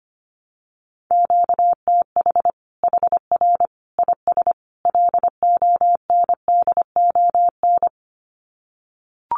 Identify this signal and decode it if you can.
Morse code, heard as a keyed tone, about 25 wpm, 700 Hz. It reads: QTH HR IS LONDON
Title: Common QSO Phrases 25wpm